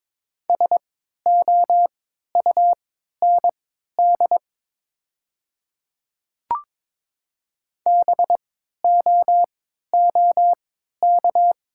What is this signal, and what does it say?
Morse code, heard as a keyed tone, about 22 wpm, spaced as for 12 wpm, 700 Hz. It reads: SOUND BOOK
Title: 3x Character Spacing - Top 500 Words - 22wpm